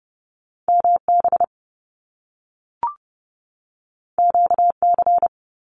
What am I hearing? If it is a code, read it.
Morse code, heard as a keyed tone, about 30 wpm, 700 Hz. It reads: MB QC
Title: Canadian Provinces and Territories 30wpm